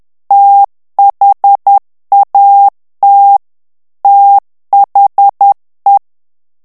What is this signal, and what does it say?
Morse code, heard as a keyed tone, about 10 wpm, 800 Hz. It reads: THAT THE
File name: ANGLAIS texte clair 3 vitesse 10 mots minute